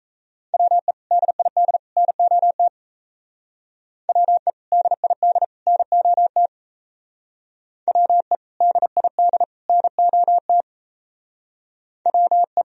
Morse code, heard as a keyed tone, about 33 wpm, 700 Hz. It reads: WE DID NOT WE DID NOT WE DID NOT WE
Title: Speed-Racing - Sets of 3 Words 28wpm